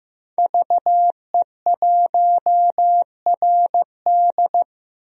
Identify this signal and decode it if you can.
Morse code, heard as a keyed tone, about 15 wpm, 700 Hz. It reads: VE1RD
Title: DXCC-100 Call Signs - 5 Characters - Repeated 3x - 15wpm